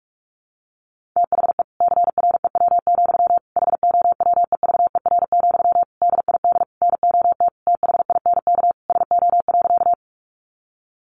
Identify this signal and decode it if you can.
Morse code, heard as a keyed tone, about 44 wpm, 700 Hz. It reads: THE CREW, HOWEVER, DID NOT THINK SO.
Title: Sentences from Top 1500 Words 45wpm